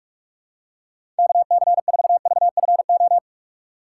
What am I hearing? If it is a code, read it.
Morse code, heard as a keyed tone, about 46 wpm, 700 Hz. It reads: KC4VFO
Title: USA Call Signs - 2x Character Spacing - 45wpm